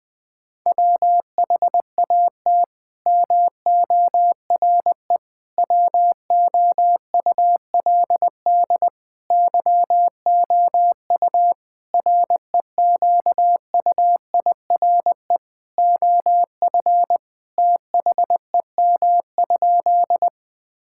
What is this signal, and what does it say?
Morse code, heard as a keyed tone, about 20 wpm, 700 Hz. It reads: WHAT MORE WOULD YOU REQUIRE OF THEM?
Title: Sentences from Top 1000 Words - Review Part 6 - 20wpm